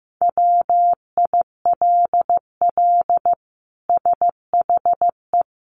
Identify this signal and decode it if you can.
Morse code, heard as a keyed tone, about 15 wpm, 700 Hz. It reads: WILL SHE
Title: Sentences from Top 1400 Words 15wpm